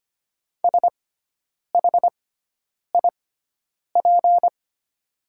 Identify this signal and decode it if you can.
Morse code, heard as a keyed tone, about 25 wpm, spaced as for 9 wpm, 690 Hz. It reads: SHIP
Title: 6x Character Spacing - Top 500 Words - 25wpm